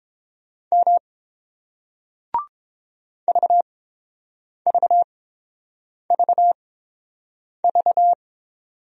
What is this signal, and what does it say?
Morse code, heard as a keyed tone, about 30 wpm, spaced as for 7 wpm, 700 Hz. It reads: M VVVV